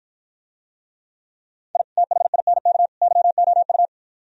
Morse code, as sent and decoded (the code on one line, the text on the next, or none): .. / - .... .. -. -.- / -.-- --- ..-
I THINK YOU